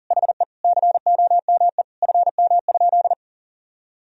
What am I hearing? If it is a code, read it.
Morse code, heard as a keyed tone, about 37 wpm, 700 Hz. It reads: HE COME FM?
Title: Sentences from Top 300 Words and Abbreviations 40wpm